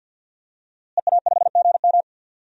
Morse code, heard as a keyed tone, about 52 wpm, 700 Hz. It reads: EA5GM
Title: DXCC-100 Call Signs - 5 Characters - Repeated 2x - 50wpm